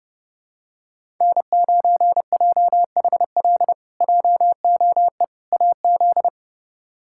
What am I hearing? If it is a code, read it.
Morse code, heard as a keyed tone, about 32 wpm, 700 Hz. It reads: N9JHL JOE AZ
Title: K1USN SST QSO Practice - 30wpm